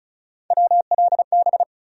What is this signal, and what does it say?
Morse code, heard as a keyed tone, about 36 wpm, 700 Hz. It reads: WLB
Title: USA Call Sign Postfixes - 3 Letters 35wpm